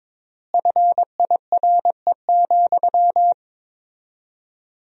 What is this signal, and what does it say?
Morse code, heard as a keyed tone, about 22 wpm, 700 Hz. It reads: FIRE,
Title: 4x Word-Spacing - Sentences from Top 400 Words - 22wpm